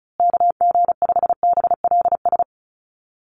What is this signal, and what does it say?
Morse code, heard as a keyed tone, about 34 wpm, 700 Hz. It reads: KG5BLS
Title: USA Call Signs Repeated 2x 35wpm